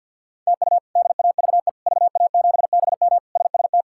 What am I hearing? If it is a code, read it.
Morse code, heard as a keyed tone, about 51 wpm, 700 Hz. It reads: TU DAVE VA7BM SST
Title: K1USN SST QSO Practice - 50wpm